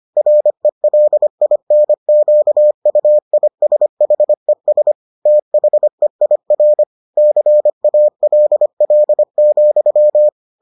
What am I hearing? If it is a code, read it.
Morse code, heard as a keyed tone, about 25 wpm, 600 Hz. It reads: RELINQUISHES THEIR CALL,